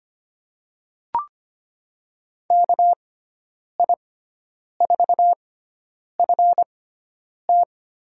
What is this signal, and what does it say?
Morse code, heard as a keyed tone, about 25 wpm, spaced as for 9 wpm, 700 Hz. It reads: KI4FT
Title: USA Call Signs - 6 Characters - 6x Character Spacing - 25wpm